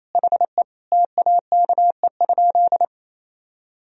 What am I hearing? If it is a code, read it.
Morse code, heard as a keyed tone, about 28 wpm, 700 Hz. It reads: HE TAKE?
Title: Sentences from Top 1000 Words - Review Part 1 - 28wpm